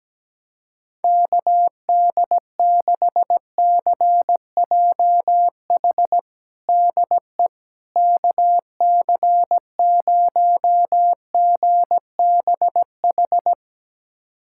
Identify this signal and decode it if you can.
Morse code, heard as a keyed tone, about 17 wpm, 700 Hz. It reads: KD6CJH DE KC0GBH
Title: USA Callsign DE Callsign 17wpm